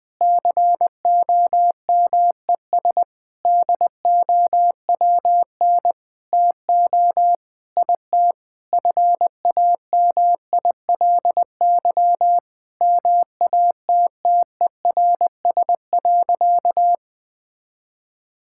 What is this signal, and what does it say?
Morse code, heard as a keyed tone, about 21 wpm, 700 Hz. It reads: COMES DOWN TO IT FAMILY MATTERS.
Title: Sentences with 6-Word Phrases - Spoken Beginning - 20wpm